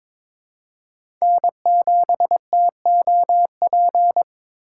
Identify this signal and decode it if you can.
Morse code, heard as a keyed tone, about 21 wpm, 700 Hz. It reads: N7TOP
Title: USA Call Signs - 5 Characters - 2x Character Spacing - 22wpm